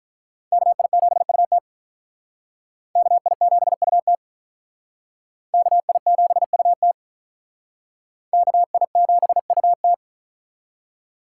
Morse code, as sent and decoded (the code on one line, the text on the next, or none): -.- .. --... ..- - / -.- .. --... ..- - / -.- .. --... ..- - / -.- .. --... ..- -
KI7UT KI7UT KI7UT KI7UT